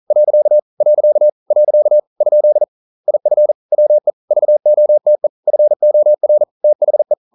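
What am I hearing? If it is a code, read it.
Morse code, heard as a keyed tone, about 41 wpm, 600 Hz. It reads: . . ? IF WE VOTE FOR THE
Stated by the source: Morse code at 40 wpm, practice text, CW med Gnister.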